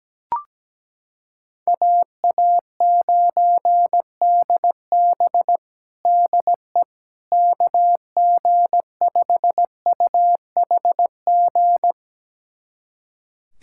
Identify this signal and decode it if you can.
Morse code, heard as a keyed tone, about 17 wpm, 700 Hz. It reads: AA9DB DE KG5UHG
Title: USA Callsign DE Callsign 17wpm